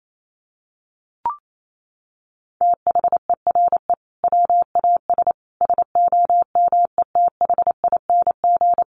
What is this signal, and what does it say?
Morse code, heard as a keyed tone, about 28 wpm, 700 Hz. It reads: THERE WAS SOMETHING
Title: Sentences from Top 1500 Words 28wpm